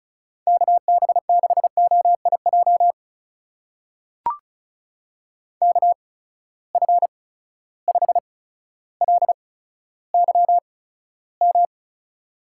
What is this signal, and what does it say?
Morse code, heard as a keyed tone, about 36 wpm, spaced as for 15 wpm, 700 Hz. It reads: KB6OIJ KF5LYM
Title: USA Call Signs - 6 Characters - 8x Character Spacing - 35wpm